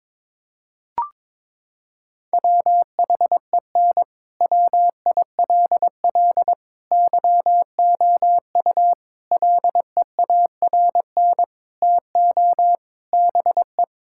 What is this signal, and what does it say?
Morse code, heard as a keyed tone, about 22 wpm, 700 Hz. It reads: WHEN WILL YOU LEARN TO BE
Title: Sentences from Top 1000 Words - Review Part 6 - 22wpm